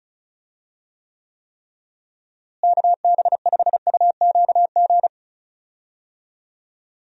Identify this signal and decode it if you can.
Morse code, heard as a keyed tone, about 36 wpm, 700 Hz. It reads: KB5UQG
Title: USA Call Signs - 6 Characters - Repeated 3x - 35wpm